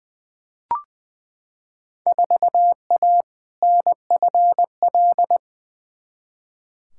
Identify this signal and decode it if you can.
Morse code, heard as a keyed tone, about 20 wpm, 700 Hz. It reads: 4A NFL
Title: Field Day Exchanges - 20wpm